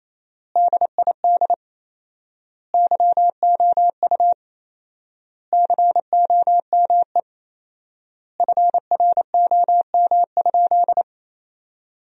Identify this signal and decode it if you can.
Morse code, heard as a keyed tone, about 28 wpm, 700 Hz. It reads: DID YOU COME FROM?
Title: 4x Word-Spacing - Sentences from Top 200 Words - 28wpm